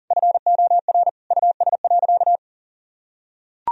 Morse code, heard as a keyed tone, about 40 wpm, 700 Hz. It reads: FOR US.
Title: Sentences from Top 1000 Words - Review Part 3 - 40wpm